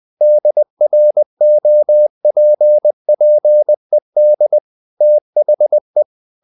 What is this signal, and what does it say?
Morse code, heard as a keyed tone, about 20 wpm, 600 Hz. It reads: DROPPED THE
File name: the old man and the sea part 2 20wpm